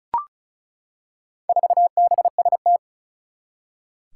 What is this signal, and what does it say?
Morse code, heard as a keyed tone, about 36 wpm, 700 Hz. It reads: 4BST
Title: Four Challenging Letter-Number Combinations 35wpm